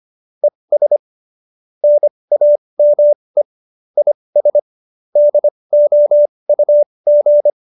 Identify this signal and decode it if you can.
Morse code, heard as a keyed tone, about 25 wpm, spaced as for 20 wpm, 590 Hz. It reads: ES NAME IS DOUG
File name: QSO303 20